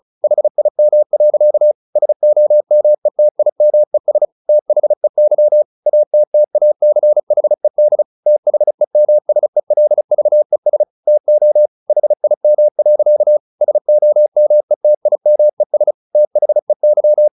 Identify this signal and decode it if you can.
Morse code, heard as a keyed tone, about 34 wpm, 600 Hz. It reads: HIM. SOMETIMES THEY ATTACHED THEMSELVES TO HIM. SOMETIMES THEY
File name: the old man and the sea 2 35wpm